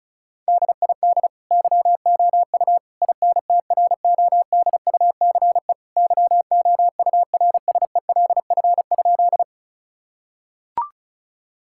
Morse code, heard as a keyed tone, about 34 wpm, 700 Hz. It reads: DID YOU INTRODUCE YOURSELF?
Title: Sentences from Top 1600 Words 35wpm